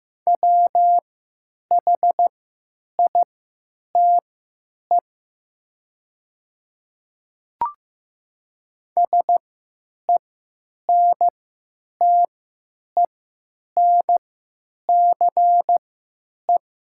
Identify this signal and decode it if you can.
Morse code, heard as a keyed tone, about 15 wpm, spaced as for 9 wpm, 700 Hz. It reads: WHITE SENTENCE
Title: 3x Character Spacing - Top 500 Words - 15wpm